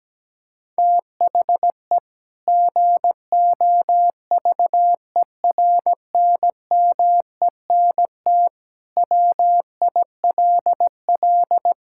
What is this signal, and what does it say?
Morse code, heard as a keyed tone, about 17 wpm, 700 Hz. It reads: THE GOVERNMENT WILL
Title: Sentences from Top 1000 Words - Review Part 5 - 17wpm